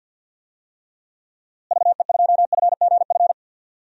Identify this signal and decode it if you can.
Morse code, heard as a keyed tone, about 50 wpm, 700 Hz. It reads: VE2FGF
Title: DXCC-100 Call Signs - 6 Characters - Repeated 2x - 50wpm